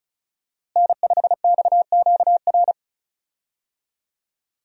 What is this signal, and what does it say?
Morse code, heard as a keyed tone, about 34 wpm, 700 Hz. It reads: N5XQR
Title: USA Call Signs - 5 Characters - Repeated 2x - 35wpm